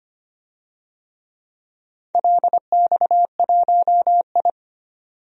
Morse code, heard as a keyed tone, about 25 wpm, 700 Hz. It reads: LX1I
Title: DXCC-100 Call Signs - 4 Characters - 25wpm